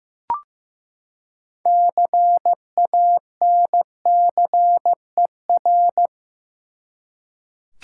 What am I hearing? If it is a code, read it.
Morse code, heard as a keyed tone, about 15 wpm, 700 Hz. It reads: CANCER
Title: Six Letter Words - 15wpm